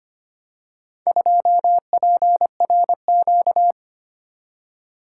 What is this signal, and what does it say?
Morse code, heard as a keyed tone, about 26 wpm, 700 Hz. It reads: 2PRQ
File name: practice / 25wpm / Four Letter-Number Combinations - 25wpm